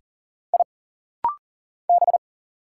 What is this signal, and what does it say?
Morse code, heard as a keyed tone, about 40 wpm, 700 Hz. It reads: I B
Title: Single Letters - Rapid-Fire 40wpm